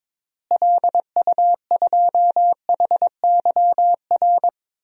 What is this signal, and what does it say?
Morse code, heard as a keyed tone, about 22 wpm, 700 Hz. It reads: LU2HYR
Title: DXCC-100 Call Signs - 6 Characters - Repeated 3x - 22wpm